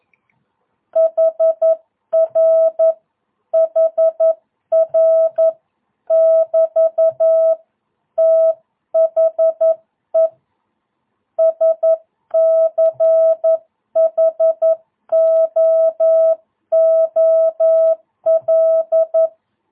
Morse code, heard as a keyed tone, about 11 wpm, 660 Hz. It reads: HR HR <BT> THE SCHOOL
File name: jh8cbh / 2022cw